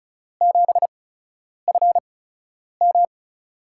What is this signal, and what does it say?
Morse code, heard as a keyed tone, about 34 wpm, spaced as for 10 wpm, 700 Hz. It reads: 7FM
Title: USA Call Signs - 5 Characters - 8x Character Spacing - 35wpm